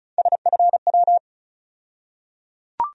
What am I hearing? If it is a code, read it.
Morse code, heard as a keyed tone, about 36 wpm, 700 Hz. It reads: SFW
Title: USA Call Sign Postfixes - 3 Letters 35wpm